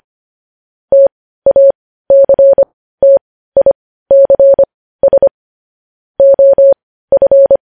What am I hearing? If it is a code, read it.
Morse code, heard as a keyed tone, about 25 wpm, spaced as for 15 wpm, 550 Hz. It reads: TACTICS OF
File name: podcast / News headlines-1x-25x15